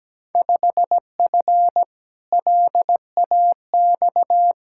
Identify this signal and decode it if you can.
Morse code, heard as a keyed tone, about 17 wpm, 700 Hz. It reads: 5F LAX
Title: Field Day Exchanges Encore - 17wpm